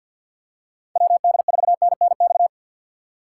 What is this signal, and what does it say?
Morse code, heard as a keyed tone, about 53 wpm, 700 Hz. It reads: W D 4 N N X
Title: USA Call Signs - 6 Characters - 8x Character Spacing - 50wpm